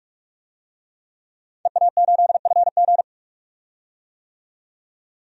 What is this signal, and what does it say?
Morse code, heard as a keyed tone, about 46 wpm, 700 Hz. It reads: EA8FG
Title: DXCC-100 Call Signs - 5 Characters - 45wpm